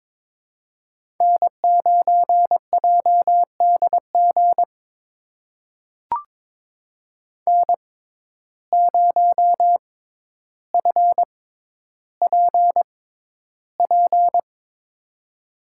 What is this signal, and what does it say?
Morse code, heard as a keyed tone, about 22 wpm, spaced as for 11 wpm, 700 Hz. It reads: N9JDG N0FPP
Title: USA Call Signs - 5 Characters - 6x Character Spacing - 22wpm